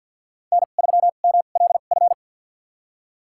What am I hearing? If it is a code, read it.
Morse code, heard as a keyed tone, about 51 wpm, spaced as for 36 wpm, 700 Hz. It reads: N3MLF
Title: USA Call Signs - 5 Characters - 2x Character Spacing - 50wpm